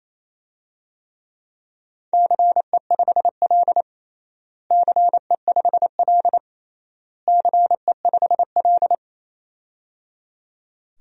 Morse code, heard as a keyed tone, about 28 wpm, 700 Hz. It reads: CE5L CE5L CE5L
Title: DXCC-100 Call Signs - 4 Characters - Repeated 3x - 28wpm